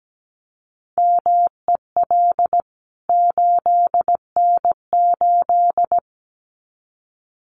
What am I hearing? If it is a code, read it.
Morse code, heard as a keyed tone, about 17 wpm, 700 Hz. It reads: MEL 8N8
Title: CWT Exchange 17wpm